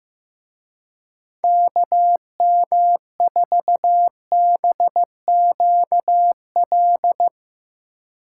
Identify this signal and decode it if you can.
Morse code, heard as a keyed tone, about 16 wpm, 700 Hz. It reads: KM4BQL
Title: Speed Racing - USA 2x3 Call Signs 15wpm